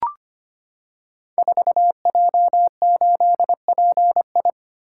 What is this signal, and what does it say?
Morse code, heard as a keyed tone, about 25 wpm, 700 Hz. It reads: 4J8PI
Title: Code Groups - 5 Letter:Numbers - 25wpm